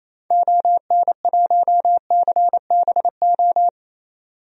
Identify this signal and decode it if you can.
Morse code, heard as a keyed tone, about 28 wpm, 700 Hz. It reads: ON1CBO
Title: DXCC-100 Call Signs - 6 Characters - 28wpm